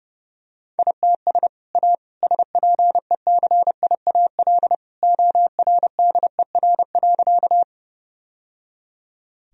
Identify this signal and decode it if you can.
Morse code, heard as a keyed tone, about 30 wpm, 700 Hz. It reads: ITS A SPECIAL ORDER.
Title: Sentences from Top 1000 Words - Review Part 3 - 30wpm